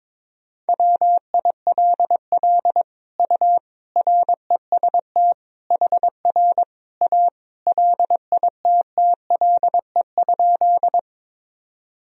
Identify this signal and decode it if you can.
Morse code, heard as a keyed tone, about 22 wpm, 700 Hz. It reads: WILL U REST HR A LITTLE?
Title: Sentences from Top 500 Words and Abbreviations 22wpm